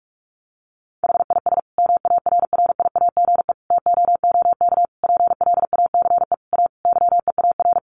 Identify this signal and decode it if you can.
Morse code, heard as a keyed tone, about 44 wpm, 700 Hz. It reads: HIS MARRIAGE TOOK PLACE A YEAR
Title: Sentences from Top 1500 Words 45wpm